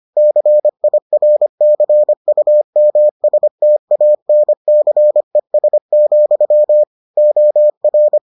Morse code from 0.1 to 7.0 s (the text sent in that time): CIRCUMSTANCES,